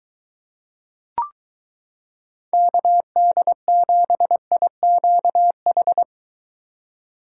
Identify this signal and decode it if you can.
Morse code, heard as a keyed tone, about 23 wpm, 700 Hz. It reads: KD7IQH